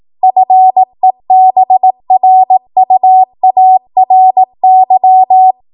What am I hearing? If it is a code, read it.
Morse code, heard as a keyed tone, about 18 wpm, 750 Hz. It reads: FEBRUARY
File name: Morse / 18WPM